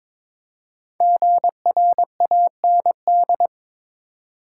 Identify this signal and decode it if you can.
Morse code, heard as a keyed tone, about 22 wpm, 700 Hz. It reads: GRAND
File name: practice / 22wpm / 5 Letter Words - First 3 Letters Given - 22wpm